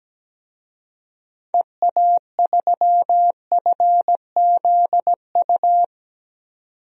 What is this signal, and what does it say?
Morse code, heard as a keyed tone, about 17 wpm, 700 Hz. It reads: EA3FZU
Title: DXCC-100 Call Signs - 6 Characters - 17wpm